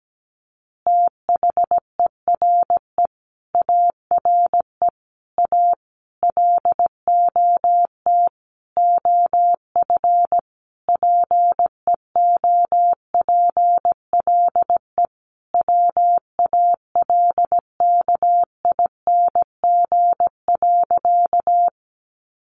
Morse code, heard as a keyed tone, about 17 wpm, 700 Hz. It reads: THERE ARE A LOT OF PEOPLE WALKING.
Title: Easy Phrases 17wpm